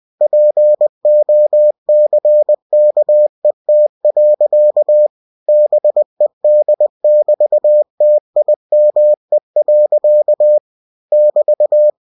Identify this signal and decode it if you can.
Morse code, heard as a keyed tone, about 20 wpm, 600 Hz. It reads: POCKET. BED<BT>TIME. <BT>
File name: cw-books / winnie-the-pooh / winnie-20wpm-page-0071